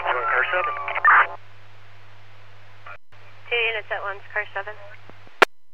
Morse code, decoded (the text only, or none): none